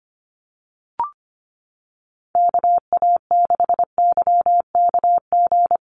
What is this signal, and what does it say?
Morse code, heard as a keyed tone, about 25 wpm, 700 Hz. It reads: KA6YKG
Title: USA Call Signs Repeated 2x 25wpm